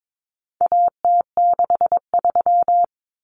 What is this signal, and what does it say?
Morse code, heard as a keyed tone, about 22 wpm, 700 Hz. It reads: AT63
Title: Common Cut Numbers - 4-Digit 22wpm